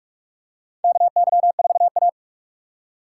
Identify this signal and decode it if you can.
Morse code, heard as a keyed tone, about 46 wpm, 700 Hz. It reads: KY4A
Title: USA Call Signs - 3x Character Spacing - 45wpm